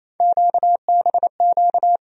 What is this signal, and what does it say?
Morse code, heard as a keyed tone, about 28 wpm, 700 Hz. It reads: QBQ
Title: USA Call Sign Postfixes - 3 Letters 28wpm